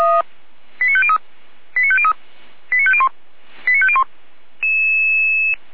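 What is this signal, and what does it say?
Five-tone selective call (ZVEI), heard as quick runs of five tones, tones of about 70 ms. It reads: 89503 89503 89501 89501
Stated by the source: ZVEI five-tone selective call received by SDR.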